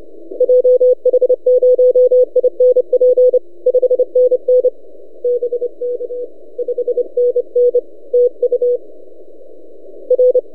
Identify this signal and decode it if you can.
Morse code, heard as a keyed tone, about 29 wpm, 500 Hz. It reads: JH0INP 5NN BK 5NN TU R
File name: t32rl-20cw